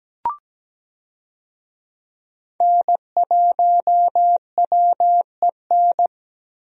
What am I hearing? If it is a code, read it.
Morse code, heard as a keyed tone, about 17 wpm, 700 Hz. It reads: N1WEN
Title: USA Call Signs - 5 Characters - Repeated 2x - 17wpm